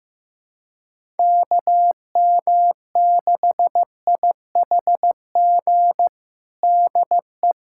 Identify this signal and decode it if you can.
Morse code, heard as a keyed tone, about 15 wpm, 700 Hz. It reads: KM6IHG DE
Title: USA Callsign DE Callsign - 6 Characters - 15wpm